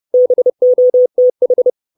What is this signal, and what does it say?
Morse code, heard as a keyed tone, about 30 wpm, 500 Hz. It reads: BOTH
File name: 632 Words 30WPM